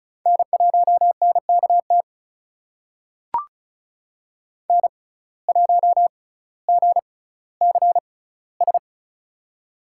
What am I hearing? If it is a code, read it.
Morse code, heard as a keyed tone, about 36 wpm, spaced as for 18 wpm, 700 Hz. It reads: N1NKT N1GCS
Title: USA Call Signs - 5 Characters - 6x Character Spacing - 35wpm